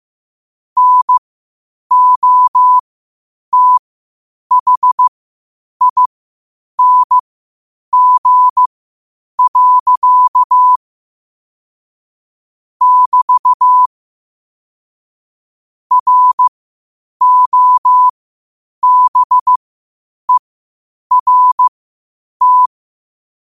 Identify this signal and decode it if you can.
Morse code, heard as a keyed tone, about 15 wpm, spaced as for 8 wpm, 1000 Hz. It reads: NOTHING. <BT> ROBERT
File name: qotdfiles / 8wpm